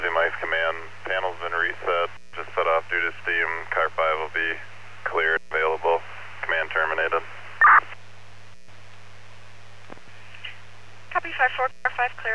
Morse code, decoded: none

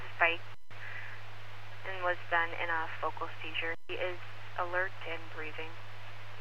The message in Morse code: none

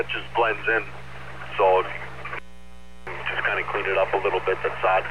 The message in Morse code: none

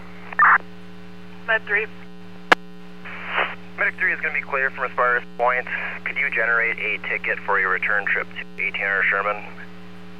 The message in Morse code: none

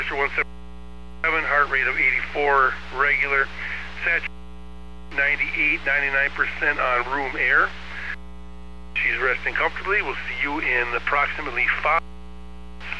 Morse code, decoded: none

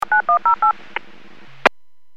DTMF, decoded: 9108

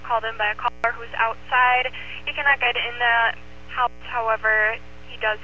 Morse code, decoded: none

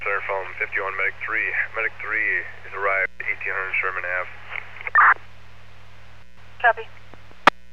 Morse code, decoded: none

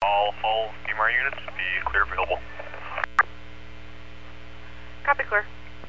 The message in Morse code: none

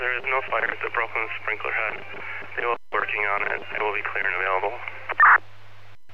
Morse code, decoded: none

